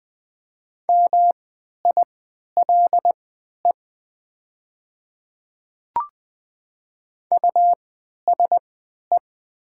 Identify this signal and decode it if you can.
Morse code, heard as a keyed tone, about 20 wpm, spaced as for 11 wpm, 700 Hz. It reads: MILE USE
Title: 3x Character Spacing - Top 300 Words - 20wpm